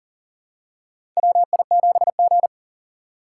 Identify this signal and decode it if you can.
Morse code, heard as a keyed tone, about 40 wpm, 700 Hz. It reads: WI7G
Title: USA Call Signs - 4 Characters - 6x Character Spacing - 40wpm